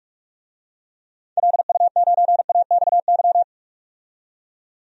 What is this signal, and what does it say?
Morse code, heard as a keyed tone, about 43 wpm, 700 Hz. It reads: LU9AXY
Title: DXCC-100 Call Signs - 6 Characters - 45wpm